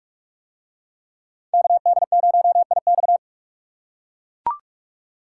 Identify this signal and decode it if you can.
Morse code, heard as a keyed tone, about 40 wpm, 700 Hz. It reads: KD0IX